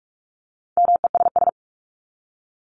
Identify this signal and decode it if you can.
Morse code, heard as a keyed tone, about 44 wpm, 700 Hz. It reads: MESS
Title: Top 1751-2000 Words 45wpm